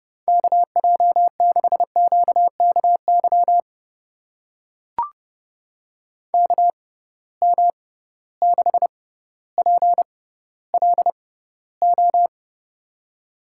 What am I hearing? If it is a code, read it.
Morse code, heard as a keyed tone, about 30 wpm, spaced as for 15 wpm, 700 Hz. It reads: KJ6QKY KM6PLO
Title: USA Call Signs - 6 Characters - 6x Character Spacing - 30wpm